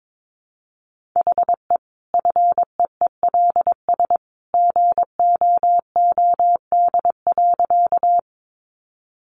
Morse code, heard as a keyed tone, about 22 wpm, 700 Hz. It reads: HE FEELS GOOD.
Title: Additional Easy Phrases 22wpm